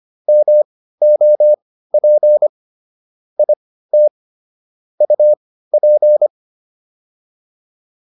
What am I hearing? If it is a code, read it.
Morse code, heard as a keyed tone, about 25 wpm, spaced as for 15 wpm, 600 Hz. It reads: MOP IT UP